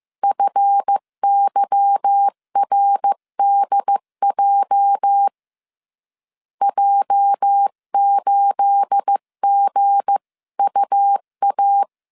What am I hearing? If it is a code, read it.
Morse code, heard as a keyed tone, about 15 wpm, 800 Hz. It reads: FYRDJ J8GUA
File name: CW / DET / 1 9 1500C S B 14wpm 3302